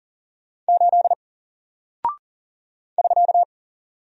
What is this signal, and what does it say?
Morse code, heard as a keyed tone, about 40 wpm, 700 Hz. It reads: 8 <SK>
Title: Single Character (Letter:Number:Punctuation:Prosign) Alt 40wpm